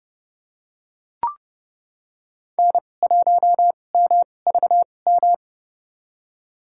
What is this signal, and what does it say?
Morse code, heard as a keyed tone, about 30 wpm, spaced as for 22 wpm, 700 Hz. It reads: N1MVM
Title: USA Call Signs - 5 Characters - 2x Character Spacing - 30wpm